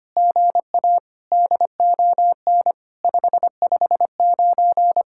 Morse code, heard as a keyed tone, about 24 wpm, 700 Hz. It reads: GA DON 559
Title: SOTA QSO Practice 25wpm